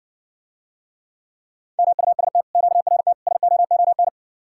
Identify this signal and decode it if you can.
Morse code, heard as a keyed tone, about 60 wpm, 700 Hz. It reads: MUST QRT SOON